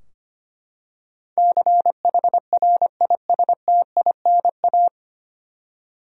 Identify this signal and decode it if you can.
Morse code, heard as a keyed tone, about 25 wpm, 700 Hz. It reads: CHRISTINA
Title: USA Common Names - 25wpm